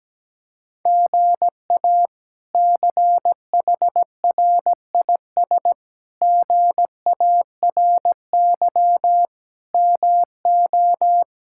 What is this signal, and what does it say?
Morse code, heard as a keyed tone, about 17 wpm, 700 Hz. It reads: GA CHRIS GARY MO
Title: K1USN SST QSO Practice - 17wpm